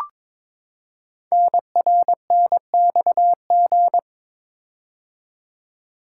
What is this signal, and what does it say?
Morse code, heard as a keyed tone, about 22 wpm, 700 Hz. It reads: NRNXG
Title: Code Groups - 5 Letters - 22wpm